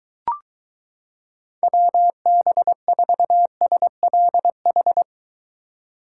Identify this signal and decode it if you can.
Morse code, heard as a keyed tone, about 23 wpm, 700 Hz. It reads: WB4SLH